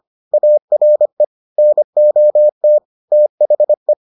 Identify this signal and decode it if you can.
Morse code, heard as a keyed tone, about 25 wpm, 600 Hz. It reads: ARE NOT THE